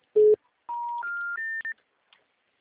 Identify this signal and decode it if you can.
Special information tone.